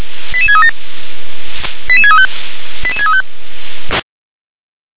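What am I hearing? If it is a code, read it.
Five-tone selective call (ZVEI), heard as quick runs of five tones, tones of about 70 ms. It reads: 88637 88635 88635
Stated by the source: ZVEI five-tone selective call received on SDR.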